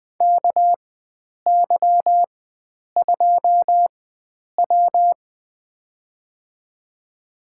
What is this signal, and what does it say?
Morse code, heard as a keyed tone, about 20 wpm, spaced as for 9 wpm, 700 Hz. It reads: KY2W
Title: USA Call Signs - 4x Character Spacing - 20wpm